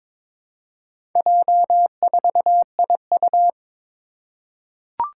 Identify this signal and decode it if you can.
Morse code, heard as a keyed tone, about 22 wpm, 700 Hz. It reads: J4IU
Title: Four Challenging Letter-Number Combinations 22wpm